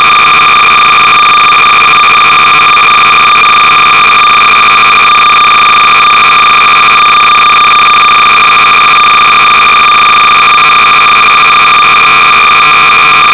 Cassette data (Kansas City type).
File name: sorcerer / 300baud / Head-on-Collision-(1980)(non-joystick-version)(L. Andres)(Quality Software)-Sorcerer-300baud